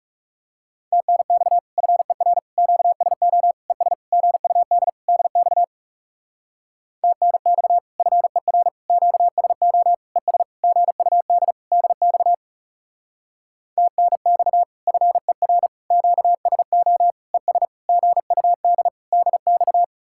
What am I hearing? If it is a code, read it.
Morse code, heard as a keyed tone, about 39 wpm, 700 Hz. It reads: TNX FER QSO ES GUD DX TNX FER QSO ES GUD DX TNX FER QSO ES GUD DX